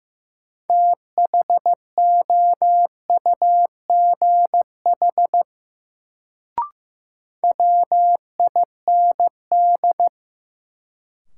Morse code, heard as a keyed tone, about 15 wpm, 700 Hz. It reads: THOUGH WIND
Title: Sentences from Top 400 Words - New Word Drill - 15wpm